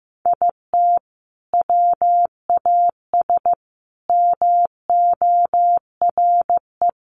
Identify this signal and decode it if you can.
Morse code, heard as a keyed tone, about 15 wpm, 700 Hz. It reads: IT WAS MORE
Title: Sentences from Top 1400 Words 15wpm